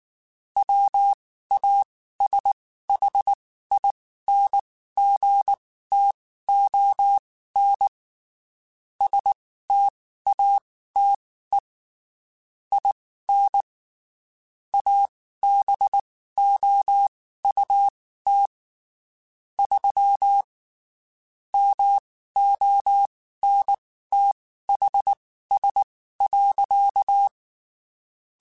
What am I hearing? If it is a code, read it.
Morse code, heard as a keyed tone, about 19 wpm, spaced as for 14 wpm, 800 Hz. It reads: WASHINGTON STATE IN ABOUT 3 MONTHS.
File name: (spaced) / QSOs Spaced-mp3s / WD0BGP 1 @18 sp